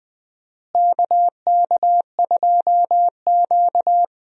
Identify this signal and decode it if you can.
Morse code, heard as a keyed tone, about 20 wpm, 700 Hz. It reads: KK2Q
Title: USA Call Signs - 4 Characters - Repeated 3x - 20wpm